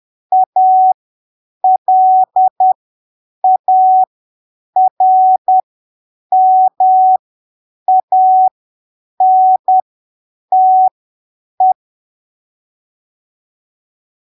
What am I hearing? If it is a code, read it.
Morse code, heard as a keyed tone, about 10 wpm, spaced as for 7 wpm, 750 Hz. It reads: ALARMANTE
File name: arquivos / texto / texto 24 10ppm